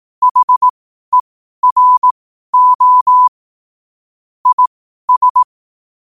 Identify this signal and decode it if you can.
Morse code, heard as a keyed tone, about 18 wpm, spaced as for 12 wpm, 1000 Hz. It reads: HERO IS
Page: qotd-at-12-wpm